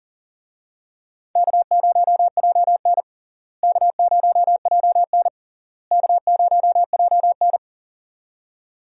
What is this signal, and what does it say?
Morse code, heard as a keyed tone, about 40 wpm, 700 Hz. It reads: K0JN K0JN K0JN
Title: USA Call Signs - 4 Characters - Repeated 3x - 40wpm